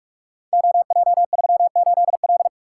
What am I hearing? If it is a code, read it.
Morse code, heard as a keyed tone, about 44 wpm, 700 Hz. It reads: OJ38L
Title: Code Groups - 5 Letter:Numbers - 45wpm